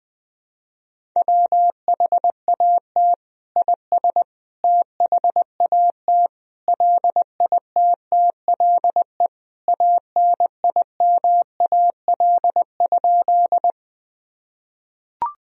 Morse code, heard as a keyed tone, about 20 wpm, 700 Hz. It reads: WHAT IS THAT LITTLE ANIMAL?
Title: Sentences from Top 1000 Words - Review Part 1 - 20wpm